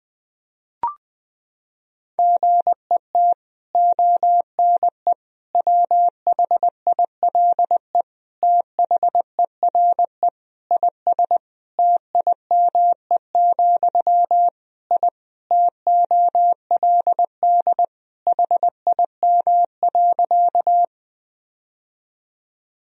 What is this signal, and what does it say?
Morse code, heard as a keyed tone, about 20 wpm, 700 Hz. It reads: GET ONE WHILE THERE IS TIME, I TOLD HIM.
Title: Sentences from Top 1000 Words - Review Part 1 - 20wpm